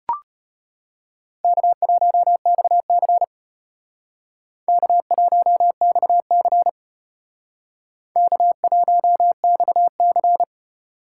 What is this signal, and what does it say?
Morse code, heard as a keyed tone, about 34 wpm, 700 Hz. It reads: K1XC K1XC K1XC